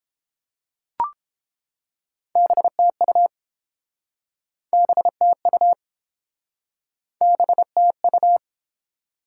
Morse code, heard as a keyed tone, about 29 wpm, 700 Hz. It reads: BTU BTU BTU